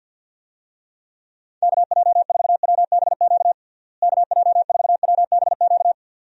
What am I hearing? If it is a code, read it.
Morse code, heard as a keyed tone, about 47 wpm, 700 Hz. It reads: KJ4WBQ KJ4WBQ
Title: USA Call Signs - 6 Characters - Repeated 2x - 50wpm